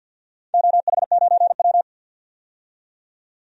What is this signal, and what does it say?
Morse code, heard as a keyed tone, about 51 wpm, 700 Hz. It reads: OH9W
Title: DXCC-100 Call Signs - 4 Characters - 50wpm